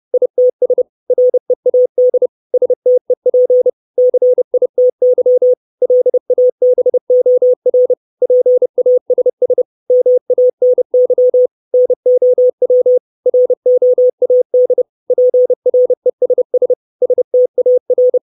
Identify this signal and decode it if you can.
Morse code, heard as a keyed tone, about 30 wpm, 500 Hz. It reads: ITS READ STEP CITY LABOR PASS MANY NOW ROAD PRESS STAR
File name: 632 Words 30WPM